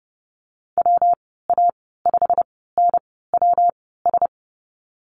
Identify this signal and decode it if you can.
Morse code, heard as a keyed tone, about 32 wpm, spaced as for 17 wpm, 700 Hz. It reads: WA5NWS
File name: Lesson 19g - Letter w - Call Signs 30wpm